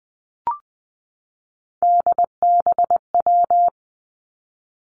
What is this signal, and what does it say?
Morse code, heard as a keyed tone, about 20 wpm, 700 Hz. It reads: DBW
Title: Three Challenging Letter-Number Combinations 20wpm